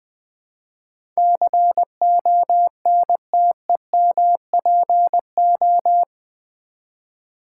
Morse code, heard as a keyed tone, about 20 wpm, 700 Hz. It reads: CONTEMPO
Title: Build A Word - 4 Letters At A Time - 12-Letter Words - 20wpm